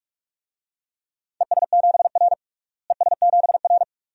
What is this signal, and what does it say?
Morse code, heard as a keyed tone, about 47 wpm, 700 Hz. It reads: ES7R ES7R
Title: DXCC-100 Call Signs - 4 Characters - Repeated 2x - 45wpm